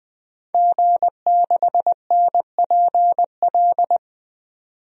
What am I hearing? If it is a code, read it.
Morse code, heard as a keyed tone, about 21 wpm, 700 Hz. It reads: G6NPL
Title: DXCC-100 Call Signs - 5 Characters - 20wpm